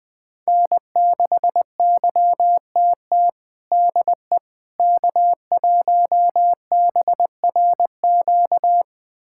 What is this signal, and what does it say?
Morse code, heard as a keyed tone, about 19 wpm, 700 Hz. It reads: N6YTT DE K1BRQ
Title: USA Callsign DE Callsign 20wpm